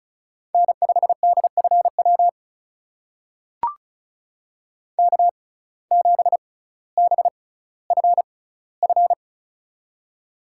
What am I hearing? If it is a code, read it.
Morse code, heard as a keyed tone, about 36 wpm, spaced as for 18 wpm, 700 Hz. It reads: N5DFW K7BFF
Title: USA Call Signs - 5 Characters - 6x Character Spacing - 35wpm